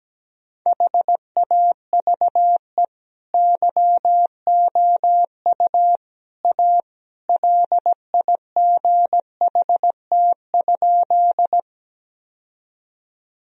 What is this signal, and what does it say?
Morse code, heard as a keyed tone, about 17 wpm, 700 Hz. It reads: HAVE YOU A LIGHT?
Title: Sentences from Top 1000 Words - Review Part 1 - 17wpm